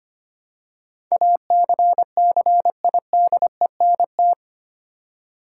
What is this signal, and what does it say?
Morse code, heard as a keyed tone, about 25 wpm, 700 Hz. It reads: ACCIDENT
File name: 8 Letter Words - First 3 Letters Given - 25wpm